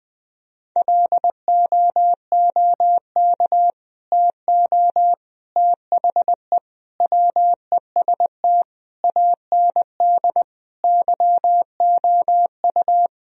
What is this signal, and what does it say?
Morse code, heard as a keyed tone, about 20 wpm, 700 Hz. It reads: LOOK TO THE WEST AND YOU
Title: Sentences from Top 1000 Words - Review Part 2 - 20wpm